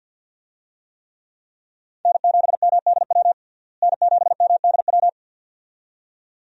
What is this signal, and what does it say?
Morse code, heard as a keyed tone, about 52 wpm, 700 Hz. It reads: N7MDW N7MDW
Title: USA Call Signs - 5 Characters - Repeated 2x - 50wpm